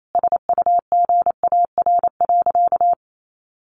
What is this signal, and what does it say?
Morse code, heard as a keyed tone, about 28 wpm, 700 Hz. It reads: SUGAR.
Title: Sentences from Top 1200 Words 28wpm